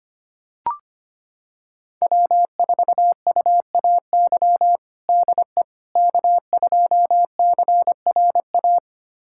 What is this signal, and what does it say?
Morse code, heard as a keyed tone, about 25 wpm, 700 Hz. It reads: W4UAY DE K2CRA
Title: USA Callsign DE Callsign - 5 Characters - 25wpm